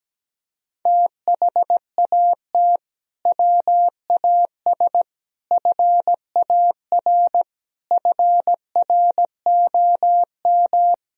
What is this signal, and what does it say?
Morse code, heard as a keyed tone, about 17 wpm, 700 Hz. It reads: THAT WAS FAR FROM
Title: Sentences from Top 1000 Words - Review Part 3 - 17wpm